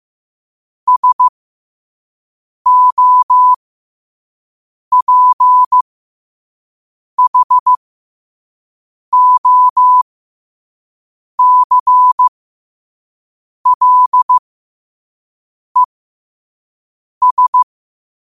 Morse code, heard as a keyed tone, about 15 wpm, spaced as for 5 wpm, 1000 Hz. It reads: SOPHOCLES
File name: qotdfiles / 5wpm